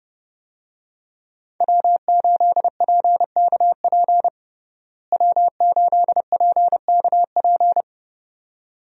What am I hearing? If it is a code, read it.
Morse code, heard as a keyed tone, about 30 wpm, 700 Hz. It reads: W8PKP W8PKP
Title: USA Call Signs - 5 Characters - Repeated 2x - 30wpm